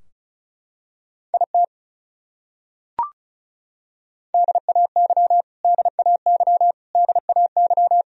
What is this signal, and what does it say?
Morse code, heard as a keyed tone, about 36 wpm, 700 Hz. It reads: IT DAY DAY DAY
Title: Top 100 Words Repeated 35wpm